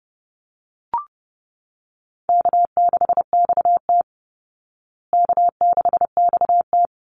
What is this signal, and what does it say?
Morse code, heard as a keyed tone, about 30 wpm, 700 Hz. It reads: K6XT K6XT
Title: Easy Call Signs Repeated 2x - 30wpm